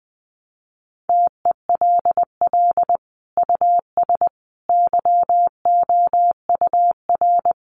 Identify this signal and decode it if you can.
Morse code, heard as a keyed tone, about 20 wpm, 700 Hz. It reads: TELL US YOUR
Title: Sentences from Top 1400 Words 20wpm